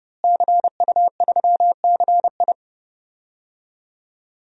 Morse code, heard as a keyed tone, about 30 wpm, 700 Hz. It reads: CU3CI
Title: DXCC-100 Call Signs - 5 Characters - Repeated 2x - 30wpm